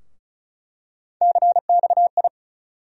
Morse code, heard as a keyed tone, about 34 wpm, 700 Hz. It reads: CXI
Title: USA Call Sign Postfixes - 3 Letters 35wpm